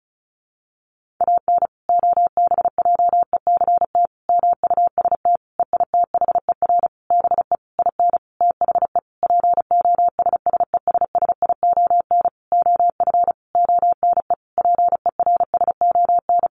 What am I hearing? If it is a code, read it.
Morse code, heard as a keyed tone, about 34 wpm, 700 Hz. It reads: AN OBJECT MUST EITHER BE IN THE POSSESSION OF ONE PERSON
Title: Sentences from Top 1400 Words 35wpm